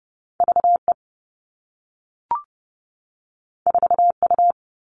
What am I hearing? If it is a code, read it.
Morse code, heard as a keyed tone, about 30 wpm, 700 Hz. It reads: V E 4 U
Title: Two Challenging Letter-Number Combinations 30wpm